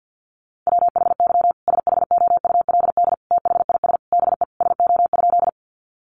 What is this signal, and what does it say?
Morse code, heard as a keyed tone, about 50 wpm, 700 Hz. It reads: WHY SHOULD THIS BE SO?
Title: Sentences from Top 300 Words 50wpm